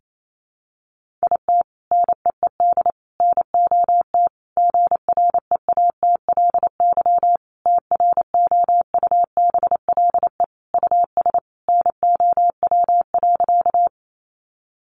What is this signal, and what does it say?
Morse code, heard as a keyed tone, about 28 wpm, 700 Hz. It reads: IT NEED NOT GREATLY TROUBLE US NOW.
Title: Sentences from Top 1400 Words 28wpm